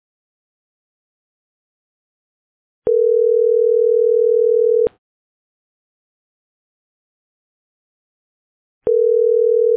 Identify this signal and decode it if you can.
Ringback tone (North America), 440 + 480 Hz.